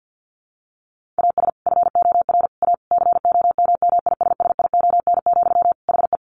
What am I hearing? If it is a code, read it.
Morse code, heard as a keyed tone, about 50 wpm, 700 Hz. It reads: AS FOR A COMMISSION, HE
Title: Sentences from Top 1400 Words 50wpm